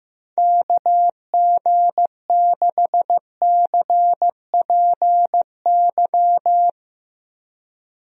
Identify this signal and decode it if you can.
Morse code, heard as a keyed tone, about 15 wpm, 700 Hz. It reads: KG6CPY
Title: USA Call Signs - 6 Characters - 15wpm